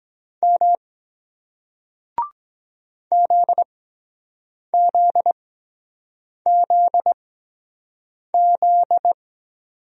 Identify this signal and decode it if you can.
Morse code, heard as a keyed tone, about 23 wpm, spaced as for 7 wpm, 700 Hz. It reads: M ZZZZ